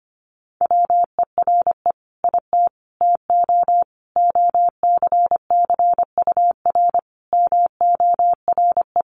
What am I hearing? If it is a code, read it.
Morse code, heard as a keyed tone, about 25 wpm, 700 Hz. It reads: WERE IT TO OCCUR MORE
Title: Sentences from Top 1300 Words 25wpm